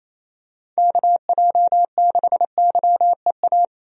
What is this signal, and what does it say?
Morse code, heard as a keyed tone, about 28 wpm, 700 Hz. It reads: KJ6YEA
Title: USA Call Signs - 6 Characters - Repeated 2x - 28wpm